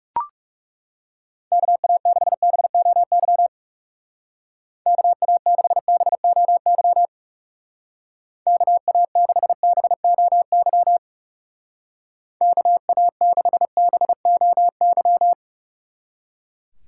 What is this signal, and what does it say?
Morse code, heard as a keyed tone, about 37 wpm, 700 Hz. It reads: KA6BOY KA6BOY KA6BOY KA6BOY
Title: Speed Racing - USA Call Signs 30wpm